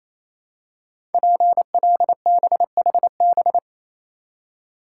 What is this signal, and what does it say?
Morse code, heard as a keyed tone, about 27 wpm, 700 Hz. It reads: PLBHB
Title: Code Groups - 5 Letter:Numbers - 28wpm